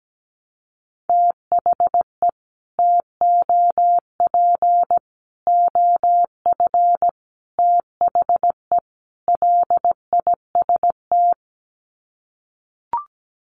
Morse code, heard as a keyed tone, about 17 wpm, 700 Hz. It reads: THE TOP OF THE LIST
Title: Sets of 5 Words Encore2 17wpm